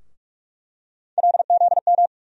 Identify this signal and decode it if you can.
Morse code, heard as a keyed tone, about 42 wpm, 700 Hz. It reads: LZM
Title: USA Call Sign Postfixes - 3 Letters 45wpm